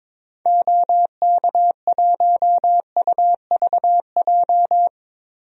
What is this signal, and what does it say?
Morse code, heard as a keyed tone, about 22 wpm, 700 Hz. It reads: OK1UVJ
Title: DXCC-100 Call Signs - 6 Characters - Repeated 2x - 22wpm